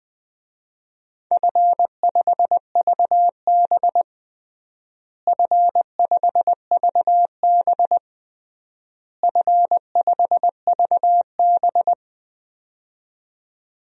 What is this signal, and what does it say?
Morse code, heard as a keyed tone, about 19 wpm, 700 Hz. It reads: F5VB F5VB F5VB
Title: DXCC-100 Call Signs - 4 Characters - Repeated 3x - 20wpm